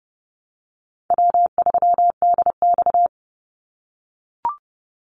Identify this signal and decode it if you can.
Morse code, heard as a keyed tone, about 30 wpm, 700 Hz. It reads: W3DX
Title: NA SOTA Activator Callsigns 30wpm